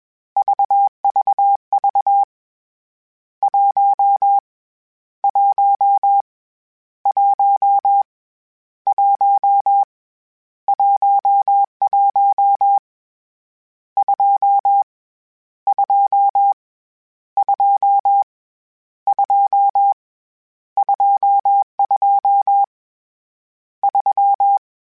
Morse code, heard as a keyed tone, about 21 wpm, 800 Hz. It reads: VVV 1 1 1 1 11 2 2 2 2 22 3
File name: Misc / Misc-mp3s / Numb @20